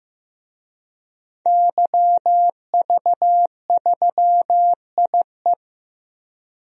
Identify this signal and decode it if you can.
Morse code, heard as a keyed tone, about 15 wpm, 700 Hz. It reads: YV3IE
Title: DXCC-100 Call Signs - 5 Characters - Repeated 2x - 15wpm